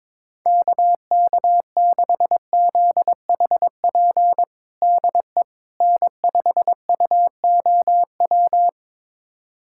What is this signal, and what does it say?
Morse code, heard as a keyed tone, about 21 wpm, 700 Hz. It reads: KK6ZHP DE N5UOW
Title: USA Callsign DE Callsign 22wpm